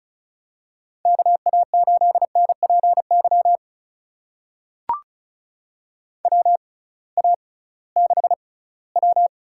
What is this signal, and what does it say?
Morse code, heard as a keyed tone, about 36 wpm, spaced as for 20 wpm, 690 Hz. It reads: KA8NPY WA6W
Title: USA Call Signs - 6 Characters - 6x Character Spacing - 35wpm